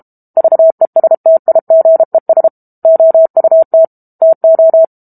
Morse code, heard as a keyed tone, about 32 wpm, 650 Hz. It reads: VESTIGES OUT TO
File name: Book Two / 06-Chapter 6 - 32 WPM